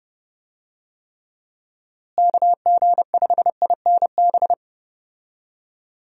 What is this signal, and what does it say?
Morse code, heard as a keyed tone, about 30 wpm, 700 Hz. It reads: KG5INB
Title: USA Call Signs - 6 Characters - Repeated 2x - 30wpm